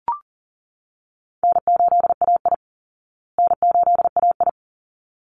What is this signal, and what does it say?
Morse code, heard as a keyed tone, about 40 wpm, 700 Hz. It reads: N8AI N8AI
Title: Easy Call Signs Repeated 2x - 40wpm